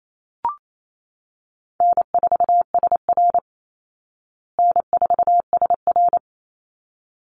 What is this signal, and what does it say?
Morse code, heard as a keyed tone, about 28 wpm, 700 Hz. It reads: N4SR N4SR
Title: Call Signs Repeated 3x - 28wpm